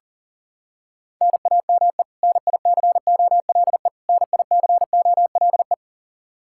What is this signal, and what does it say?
Morse code, heard as a keyed tone, about 40 wpm, 700 Hz. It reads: NAME NICOLE NICOLE
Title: Practice QSO - Exchange 1 - Beginning - 40wpm